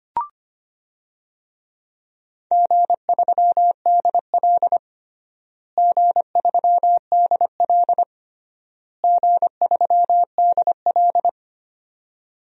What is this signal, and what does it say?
Morse code, heard as a keyed tone, about 25 wpm, 700 Hz. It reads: G3DL G3DL G3DL
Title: DXCC-100 Call Signs - 4 Characters - Repeated 3x - 25wpm